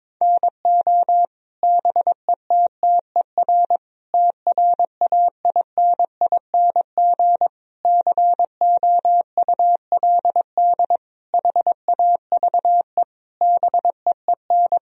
Morse code, heard as a keyed tone, about 22 wpm, 700 Hz. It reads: NO BETTER TRAINING COULD HAVE BEEN
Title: Sentences from Top 1000 Words - Review Part 4 - 22wpm